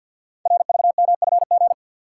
Morse code, heard as a keyed tone, about 47 wpm, 700 Hz. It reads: RVMFG
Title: Code Groups - 5 Letter:Numbers - 50wpm